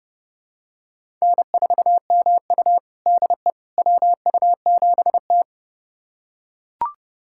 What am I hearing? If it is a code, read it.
Morse code, heard as a keyed tone, about 32 wpm, 700 Hz. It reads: N4MU DE WU7T
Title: USA Callsign DE Callsign - 4 Characters - 30wpm